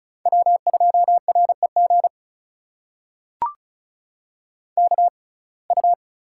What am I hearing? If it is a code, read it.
Morse code, heard as a keyed tone, about 36 wpm, spaced as for 25 wpm, 700 Hz. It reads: W2REG KU
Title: USA Call Signs - 5 Characters - 6x Character Spacing - 35wpm